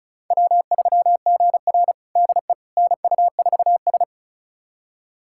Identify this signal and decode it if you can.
Morse code, heard as a keyed tone, about 36 wpm, 700 Hz. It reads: W3GR DE NU4S
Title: USA Callsign DE Callsign - 4 Characters - 35wpm